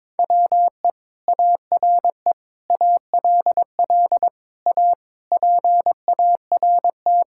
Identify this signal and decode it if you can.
Morse code, heard as a keyed tone, about 22 wpm, 700 Hz. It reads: WE ARE ALL A PART
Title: Sentences from Top 1000 Words - Review Part 6 - 22wpm